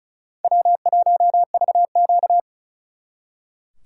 Morse code, heard as a keyed tone, about 45 wpm, 700 Hz. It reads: W1VQ